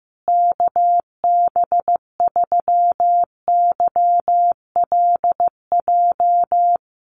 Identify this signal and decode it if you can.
Morse code, heard as a keyed tone, about 15 wpm, 700 Hz. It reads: KB3YLJ
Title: USA Call Signs Repeated 3x 15wpm